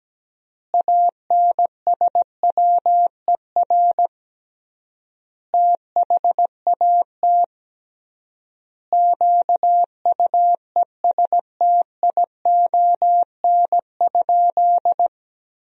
Morse code, heard as a keyed tone, about 17 wpm, 700 Hz. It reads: ANSWER THAT QUESTION?
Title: 3x Word-Spacing - Sentences from Top 400 Words - 17wpm